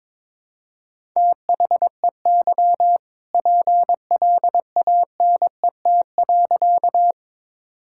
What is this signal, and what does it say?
Morse code, heard as a keyed tone, about 23 wpm, 700 Hz. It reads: THEY PLANET.
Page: morse-code-ninja-practice-22wpm